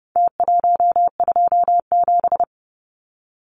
Morse code, heard as a keyed tone, about 28 wpm, 700 Hz. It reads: T127
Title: Numbers - Contest Style - 4-Digit 30wpm